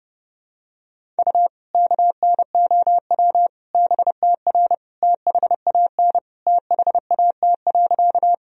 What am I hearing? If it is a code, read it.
Morse code, heard as a keyed tone, about 30 wpm, 700 Hz. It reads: U KNOW BTR THAN THAT.
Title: Sentences from Top 400 Words and Abbreviations 30wpm